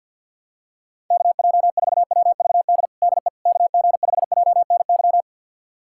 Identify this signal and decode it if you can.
Morse code, heard as a keyed tone, about 50 wpm, 700 Hz. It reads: KJ4WVD DE KG5JNY
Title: USA Callsign DE Callsign - 6 Characters - 50wpm